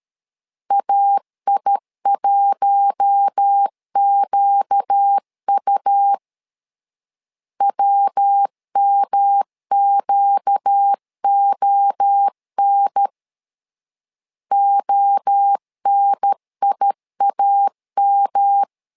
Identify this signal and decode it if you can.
Morse code, heard as a keyed tone, about 13 wpm, 800 Hz. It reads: AI1QU WMQON ONIAM
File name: LEZIONI / 4 Q1 12wpm 2028